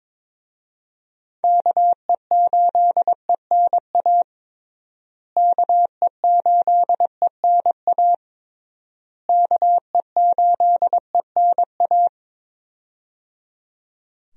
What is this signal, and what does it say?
Morse code, heard as a keyed tone, about 22 wpm, 700 Hz. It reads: KE8ENA KE8ENA KE8ENA
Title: USA Call Signs - 6 Characters - Repeated 3x - 22wpm